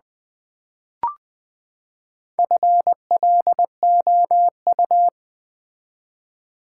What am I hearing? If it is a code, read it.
Morse code, heard as a keyed tone, about 20 wpm, 700 Hz. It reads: FLOU
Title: Build A Word - 4 Letters At A Time - 8-Letter Words - 20wpm